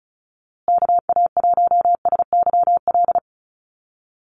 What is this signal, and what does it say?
Morse code, heard as a keyed tone, about 34 wpm, 700 Hz. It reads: KA1SYL
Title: USA Call Signs Repeated 3x 35wpm